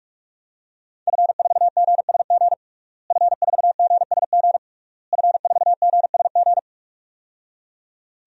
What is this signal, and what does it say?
Morse code, heard as a keyed tone, about 46 wpm, 700 Hz. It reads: F4GSG F4GSG F4GSG
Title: DXCC-100 Call Signs - 5 Characters - Repeated 3x - 45wpm